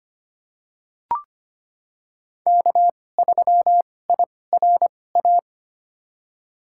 Morse code, heard as a keyed tone, about 25 wpm, spaced as for 18 wpm, 700 Hz. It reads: K3IRA
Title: USA Call Signs - 5 Characters - 2x Character Spacing - 25wpm